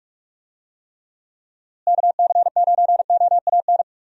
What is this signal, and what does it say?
Morse code, heard as a keyed tone, about 42 wpm, 700 Hz. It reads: KC9OAN
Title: USA Call Signs - 6 Characters - Repeated 3x - 45wpm